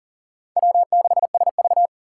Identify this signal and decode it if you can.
Morse code, heard as a keyed tone, about 51 wpm, 700 Hz. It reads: W6SV